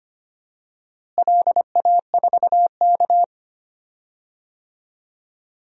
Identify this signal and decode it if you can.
Morse code, heard as a keyed tone, about 25 wpm, 700 Hz. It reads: LA4K
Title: DXCC-100 Call Signs - 4 Characters - Repeated 2x - 25wpm